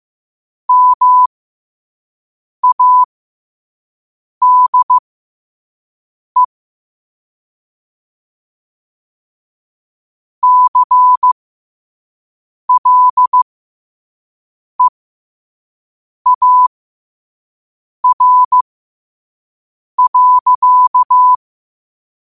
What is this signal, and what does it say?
Morse code, heard as a keyed tone, about 15 wpm, spaced as for 5 wpm, 1000 Hz. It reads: MADE CLEAR.
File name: qotdfiles / 5wpm